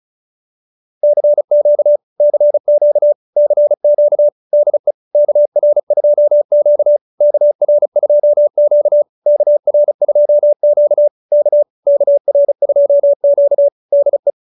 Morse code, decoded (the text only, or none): CQ CQ CQ DE KR2Q KR2Q KR2Q K KR2Q DE